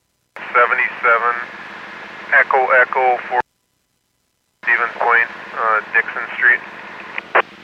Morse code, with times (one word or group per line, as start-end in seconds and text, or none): none